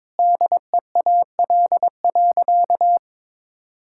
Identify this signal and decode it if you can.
Morse code, heard as a keyed tone, about 22 wpm, 700 Hz. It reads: DEAL.
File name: Sentences with 7-Word Phrases - 22wpm